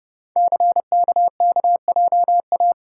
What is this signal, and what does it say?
Morse code, heard as a keyed tone, about 29 wpm, 700 Hz. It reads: CKKJA
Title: Code Groups - 5 Letters - 30wpm